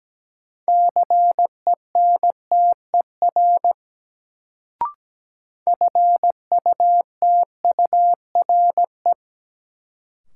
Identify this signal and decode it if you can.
Morse code, heard as a keyed tone, about 17 wpm, 700 Hz. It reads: CENTER FUTURE
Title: Sets of 3 Words - New Word Drill - 17wpm